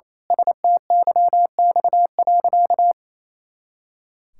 Morse code, heard as a keyed tone, about 27 wpm, 700 Hz. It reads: STYX.
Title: Pangram Challenge v2 - 28wpm